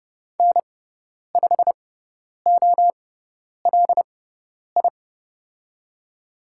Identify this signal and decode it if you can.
Morse code, heard as a keyed tone, about 30 wpm, spaced as for 10 wpm, 700 Hz. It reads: N5OLI
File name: Lesson 17g - Letter u - Call Signs 30wpm